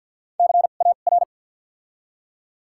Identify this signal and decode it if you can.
Morse code, heard as a keyed tone, about 53 wpm, spaced as for 37 wpm, 700 Hz. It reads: CAR
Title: 2x Character Spacing - Top 500 Words - 50wpm